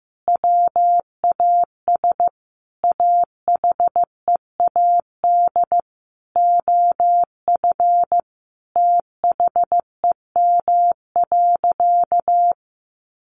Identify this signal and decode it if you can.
Morse code, heard as a keyed tone, about 15 wpm, 700 Hz. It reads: WAS AHEAD OF THEM.
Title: Sentences from Top 1100 Words 15wpm